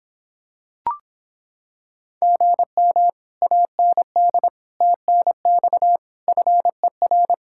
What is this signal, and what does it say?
Morse code, heard as a keyed tone, about 26 wpm, 700 Hz. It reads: GM AND TNX FER